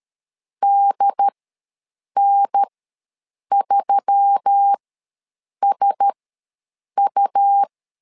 Morse code, heard as a keyed tone, about 13 wpm, spaced as for 7 wpm, 800 Hz. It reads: DN3SU
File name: LEZIONI / 4 Q1S3 ws 12wpm 2038